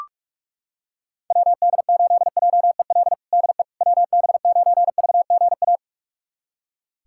Morse code, heard as a keyed tone, about 42 wpm, 700 Hz. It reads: WD8JEL DE WB9VGA
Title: USA Callsign DE Callsign - 6 Characters - 45wpm